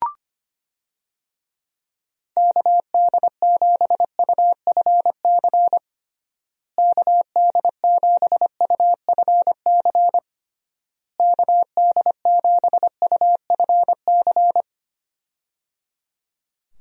Morse code, heard as a keyed tone, about 25 wpm, 700 Hz. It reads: KD7UFC KD7UFC KD7UFC
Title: USA Call Signs - 6 Characters - Repeated 3x - 25wpm